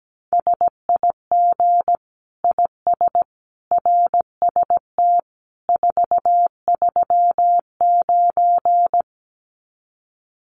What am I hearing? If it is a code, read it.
Morse code, heard as a keyed tone, about 17 wpm, 700 Hz. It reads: SIG IS RST 439
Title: Common QSO Phrases 17wpm